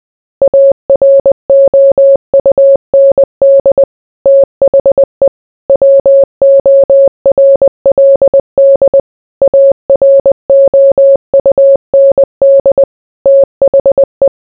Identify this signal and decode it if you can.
Morse code, heard as a keyed tone, about 20 wpm, 550 Hz. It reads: AROUND THE WORLD AROUND THE